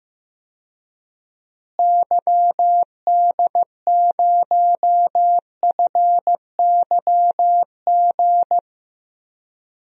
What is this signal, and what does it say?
Morse code, heard as a keyed tone, about 15 wpm, 700 Hz. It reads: YD0FYG
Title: DXCC-100 Call Signs - 6 Characters - Repeated 2x - 15wpm